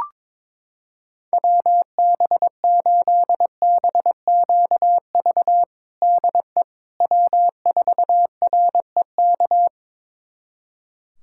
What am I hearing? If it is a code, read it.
Morse code, heard as a keyed tone, about 22 wpm, 700 Hz. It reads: WB8BQV DE W4REK
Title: USA Callsign DE Callsign 22wpm